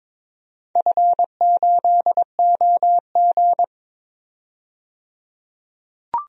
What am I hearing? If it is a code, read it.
Morse code, heard as a keyed tone, about 21 wpm, 700 Hz. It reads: F8OG
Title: DXCC-100 Call Signs - 4 Characters - 22wpm